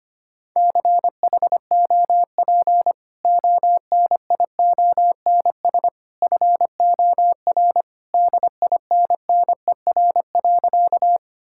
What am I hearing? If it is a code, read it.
Morse code, heard as a keyed tone, about 25 wpm, 700 Hz. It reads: CHOP ONIONS FOR DINNER.
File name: Easy Sentences with Hints - 25wpm